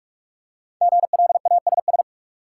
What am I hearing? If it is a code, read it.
Morse code, heard as a keyed tone, about 46 wpm, 700 Hz. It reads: GLASS
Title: Five Letter Words - 45wpm